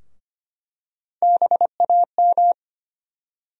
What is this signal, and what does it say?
Morse code, heard as a keyed tone, about 25 wpm, 700 Hz. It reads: BAM
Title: USA Call Sign Postfixes - 3 Letters 25wpm